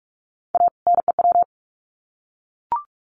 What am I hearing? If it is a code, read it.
Morse code, heard as a keyed tone, about 44 wpm, 700 Hz. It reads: A NEW
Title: Sets of 2 Words 45wpm